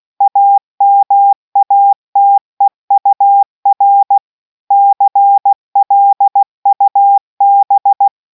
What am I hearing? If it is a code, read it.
Morse code, heard as a keyed tone, about 16 wpm, 800 Hz. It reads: AMATEUR CLUB